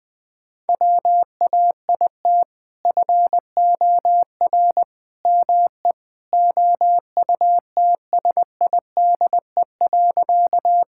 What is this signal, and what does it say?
Morse code, heard as a keyed tone, about 21 wpm, 700 Hz. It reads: WAIT FOR ME OUTSIDE.
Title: Sentences from Top 1000 Words - Review Part 6 - 20wpm